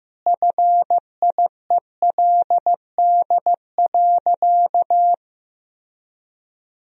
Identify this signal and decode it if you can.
Morse code, heard as a keyed tone, about 16 wpm, 700 Hz. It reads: FIELD.
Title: Sentences from Top 1000 Words - Review Part 2 - 15wpm